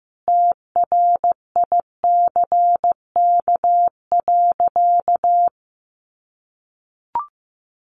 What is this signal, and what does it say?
Morse code, heard as a keyed tone, about 15 wpm, 700 Hz. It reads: TRICK.
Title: Sentences from Top 1500 Words 15wpm